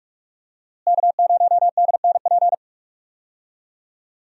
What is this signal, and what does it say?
Morse code, heard as a keyed tone, about 41 wpm, 700 Hz. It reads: K0DNP
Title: USA Call Signs - 5 Characters - Repeated 2x - 45wpm